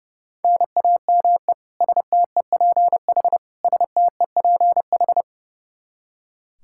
Morse code, heard as a keyed tone, about 29 wpm, 700 Hz. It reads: NAME STEPH STEPH
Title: Practice QSO - Exchange 1 - Beginning - 30wpm